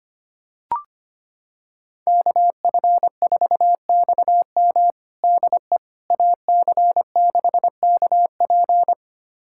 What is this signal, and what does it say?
Morse code, heard as a keyed tone, about 25 wpm, 700 Hz. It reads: KF4XM DE AC6KP
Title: USA Callsign DE Callsign - 5 Characters - 25wpm